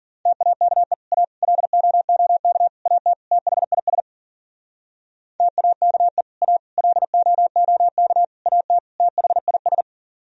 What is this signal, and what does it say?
Morse code, heard as a keyed tone, about 51 wpm, 700 Hz. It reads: TAKE A LOOK AT THIS TAKE A LOOK AT THIS